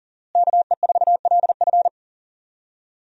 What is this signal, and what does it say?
Morse code, heard as a keyed tone, about 40 wpm, 700 Hz. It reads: KE4LF
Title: USA Call Signs - 5 Characters - Repeated 2x - 40wpm